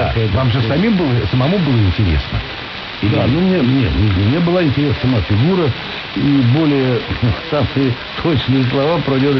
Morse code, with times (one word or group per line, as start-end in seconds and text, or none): none